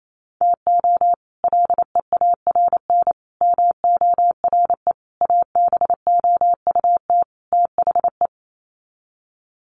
Morse code, heard as a keyed tone, about 28 wpm, 700 Hz. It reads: TO LEARN MORE ABOUT THE
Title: Sets of 5 Words Encore2 28wpm